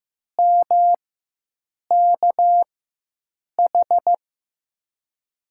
Morse code, heard as a keyed tone, about 15 wpm, spaced as for 7 wpm, 700 Hz. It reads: MKH
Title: USA Call Signs - 6 Characters - 4x Character Spacing - 15wpm